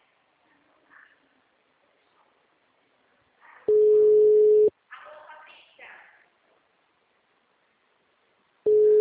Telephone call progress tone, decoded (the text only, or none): ringback tone